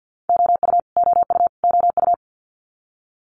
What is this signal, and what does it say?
Morse code, heard as a keyed tone, about 50 wpm, 700 Hz. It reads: OU OU OU
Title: Multi-letter Phonemes Repeated 3x 50wpm